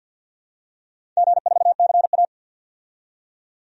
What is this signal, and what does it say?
Morse code, heard as a keyed tone, about 47 wpm, 700 Hz. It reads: G4CA
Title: DXCC-100 Call Signs - 4 Characters - 50wpm